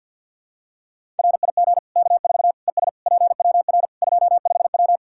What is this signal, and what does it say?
Morse code, heard as a keyed tone, about 47 wpm, 700 Hz. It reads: RIG K4 ES PWR 25W
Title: Practice QSO - Exchange 2 - Info - 50wpm